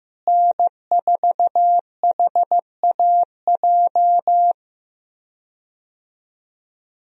Morse code, heard as a keyed tone, about 15 wpm, 700 Hz. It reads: N4HAJ
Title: USA Call Signs - 5 Characters - Repeated 3x - 15wpm